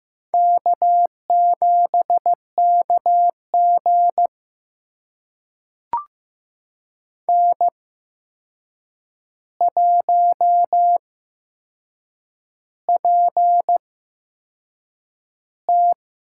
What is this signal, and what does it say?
Morse code, heard as a keyed tone, about 15 wpm, spaced as for 6 wpm, 700 Hz. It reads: K7KG N1PT
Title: USA Call Signs - 4 Characters - 8x Character Spacing - 15wpm